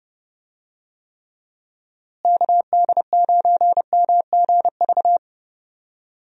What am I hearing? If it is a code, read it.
Morse code, heard as a keyed tone, about 32 wpm, 700 Hz. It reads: KD9MGV
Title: USA Call Signs - 6 Characters - Repeated 2x - 30wpm